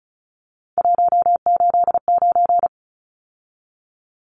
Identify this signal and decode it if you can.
Morse code, heard as a keyed tone, about 34 wpm, 700 Hz. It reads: 189
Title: Numbers - 3-Digit 35wpm